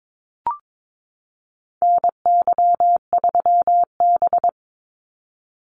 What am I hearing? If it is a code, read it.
Morse code, heard as a keyed tone, about 22 wpm, 700 Hz. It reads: NY3B
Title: Call Signs Repeated 3x - 22wpm